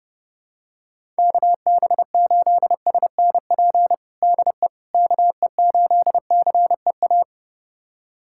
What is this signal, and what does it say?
Morse code, heard as a keyed tone, about 32 wpm, 700 Hz. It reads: KB8SNP DE KE8CEA
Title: USA Callsign DE Callsign - 6 Characters - 30wpm